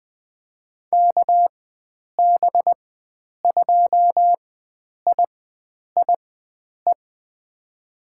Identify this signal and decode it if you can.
Morse code, heard as a keyed tone, about 20 wpm, spaced as for 9 wpm, 700 Hz. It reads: KB2IIE
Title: USA Call Signs - 6 Characters - 4x Character Spacing - 20wpm